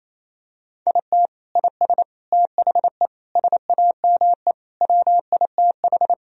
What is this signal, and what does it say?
Morse code, heard as a keyed tone, about 27 wpm, 700 Hz. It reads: IT IS THE SAME WITH
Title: Sentences from Top 1000 Words - Review Part 3 - 28wpm